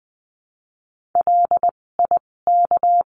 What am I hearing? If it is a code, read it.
Morse code, heard as a keyed tone, about 19 wpm, spaced as for 16 wpm, 700 Hz. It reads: LIK
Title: 256 Lesson 42a - Words 12-20wpm